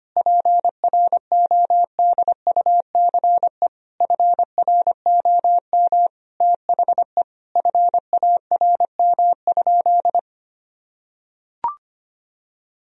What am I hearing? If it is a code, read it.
Morse code, heard as a keyed tone, about 25 wpm, 700 Hz. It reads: PRODUCE FROM THE FARM?
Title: Sentences from Top 1000 Words - Review Part 3 - 25wpm